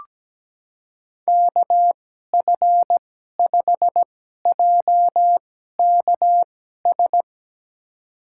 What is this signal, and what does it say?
Morse code, heard as a keyed tone, about 17 wpm, spaced as for 12 wpm, 700 Hz. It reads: KF5JKS
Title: USA Call Signs - 2x Character Spacing - 17wpm